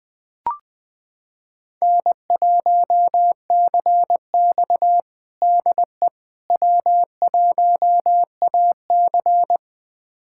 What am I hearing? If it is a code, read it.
Morse code, heard as a keyed tone, about 20 wpm, 700 Hz. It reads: N1CX DE W1AC
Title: USA Callsign DE Callsign - 4 Characters - 20wpm